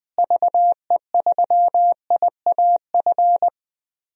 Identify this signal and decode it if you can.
Morse code, heard as a keyed tone, about 20 wpm, 700 Hz. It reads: VE3IAF
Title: DXCC-100 Call Signs - 6 Characters - Repeated 2x - 20wpm